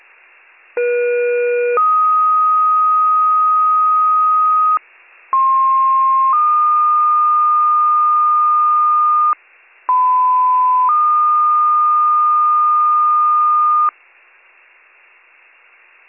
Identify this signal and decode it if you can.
Two-tone page, heard as a one-second tone, then a three-second tone, about 480 then 1250 Hz, about 1060 then 1220 Hz, about 1010 then 1220 Hz. It is heard from 0.8 to 13.9 s.